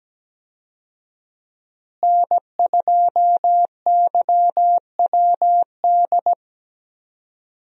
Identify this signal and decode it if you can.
Morse code, heard as a keyed tone, about 17 wpm, 700 Hz. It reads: N2YWD
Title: USA Call Signs - 5 Characters - Repeated 3x - 17wpm